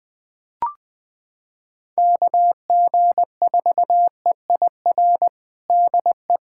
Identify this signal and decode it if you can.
Morse code, heard as a keyed tone, about 20 wpm, 700 Hz. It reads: KG4EIR DE
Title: USA Callsign DE Callsign 20wpm